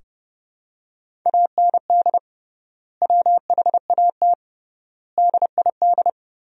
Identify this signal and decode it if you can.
Morse code, heard as a keyed tone, about 32 wpm, 700 Hz. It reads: AND WHAT DID
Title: 3x Word-Spacing - Sentences from Top 200 Words - 30wpm